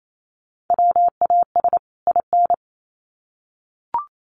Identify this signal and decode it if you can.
Morse code, heard as a keyed tone, about 28 wpm, 700 Hz. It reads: WAS IN
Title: Sets of 2 Words 28wpm